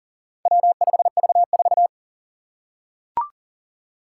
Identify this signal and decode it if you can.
Morse code, heard as a keyed tone, about 38 wpm, 700 Hz. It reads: W5V4
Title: Four Challenging Letter-Number Combinations 40wpm